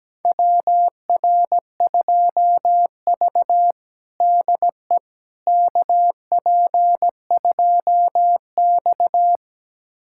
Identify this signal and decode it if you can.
Morse code, heard as a keyed tone, about 17 wpm, 700 Hz. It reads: WR2V DE KP2X
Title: USA Callsign DE Callsign - 4 Characters - 17wpm